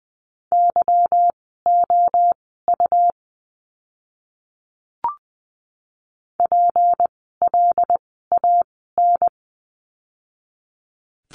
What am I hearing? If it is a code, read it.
Morse code, heard as a keyed tone, about 20 wpm, spaced as for 14 wpm, 700 Hz. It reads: YOU PLAN
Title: 153 Lesson 25f - Letter p - Words 20wpm